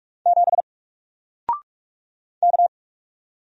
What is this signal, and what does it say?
Morse code, heard as a keyed tone, about 44 wpm, 700 Hz. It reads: 7 K
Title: Single Letter-Number Alt 45wpm